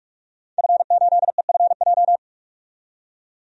Morse code, heard as a keyed tone, about 44 wpm, 700 Hz. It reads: F8EFJ
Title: DXCC-100 Call Signs - 5 Characters - 45wpm